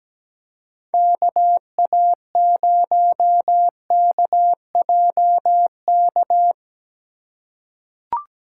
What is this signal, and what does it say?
Morse code, heard as a keyed tone, about 17 wpm, 700 Hz. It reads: KA0KJK
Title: USA Call Signs - 6 Characters - 2x Character Spacing - 17wpm